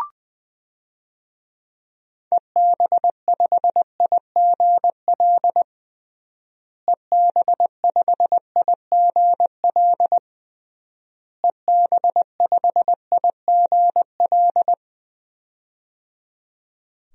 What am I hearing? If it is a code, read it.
Morse code, heard as a keyed tone, about 21 wpm, 700 Hz. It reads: EB5IGL EB5IGL EB5IGL
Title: DXCC-100 Call Signs - 6 Characters - Repeated 3x - 20wpm